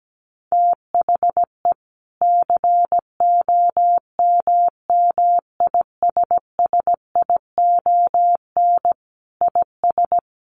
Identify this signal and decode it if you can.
Morse code, heard as a keyed tone, about 17 wpm, 700 Hz. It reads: THE COMMISSION IS
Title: Sentences from Top 1400 Words 17wpm